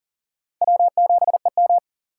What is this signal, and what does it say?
Morse code, heard as a keyed tone, about 40 wpm, 700 Hz. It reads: W7EM
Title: USA Call Signs - 2x Character Spacing - 40wpm